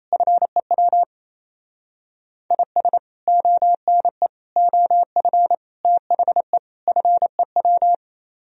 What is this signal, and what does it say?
Morse code, heard as a keyed tone, about 37 wpm, 700 Hz. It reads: FEW IS ONE OF THE FEW